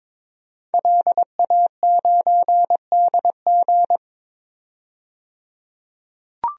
Morse code, heard as a keyed tone, about 21 wpm, 700 Hz. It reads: LA9DG
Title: DXCC-100 Call Signs - 5 Characters - 22wpm